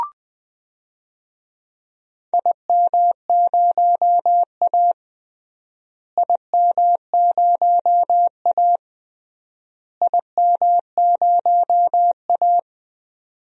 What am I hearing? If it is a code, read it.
Morse code, heard as a keyed tone, about 20 wpm, 700 Hz. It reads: IM0A IM0A IM0A
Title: DXCC-100 Call Signs - 4 Characters - Repeated 3x - 20wpm